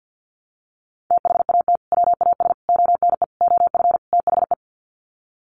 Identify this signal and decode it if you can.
Morse code, heard as a keyed tone, about 49 wpm, 700 Hz. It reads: THAT WAS ONE OF THE
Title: Sets of 5 Words 50wpm